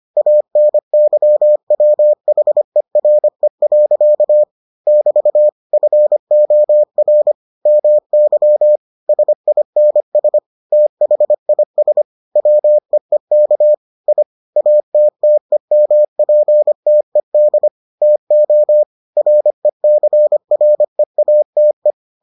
Morse code, heard as a keyed tone, about 25 wpm, 600 Hz. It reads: ANYWHERE. <BT> FOR MY SINS THIS WEEK I ATTEMPTED TO RECREATE